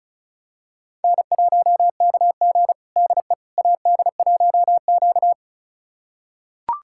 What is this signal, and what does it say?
Morse code, heard as a keyed tone, about 36 wpm, 700 Hz. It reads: N1KG DE AD1Q
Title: USA Callsign DE Callsign - 4 Characters - 35wpm